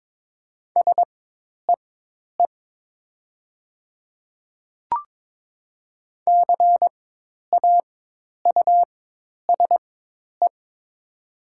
Morse code, heard as a keyed tone, about 22 wpm, spaced as for 10 wpm, 690 Hz. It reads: SEE CAUSE
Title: 4x Character Spacing - Top 200 Words - 22wpm